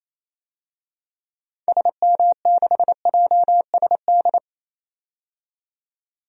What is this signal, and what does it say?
Morse code, heard as a keyed tone, about 28 wpm, 700 Hz. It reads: SM6JSD
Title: DXCC-100 Call Signs - 6 Characters - 28wpm